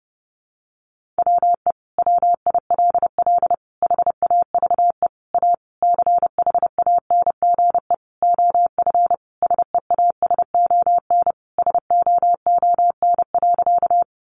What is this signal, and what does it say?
Morse code, heard as a keyed tone, about 28 wpm, 700 Hz. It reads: WE WILL HAVE A CHANGE OF SEASON SOON.
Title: Sentences from Top 1400 Words 30wpm